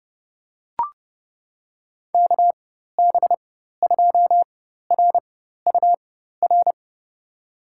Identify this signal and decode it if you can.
Morse code, heard as a keyed tone, about 30 wpm, spaced as for 14 wpm, 700 Hz. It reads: KB2RUR
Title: USA Call Signs - 6 Characters - 4x Character Spacing - 30wpm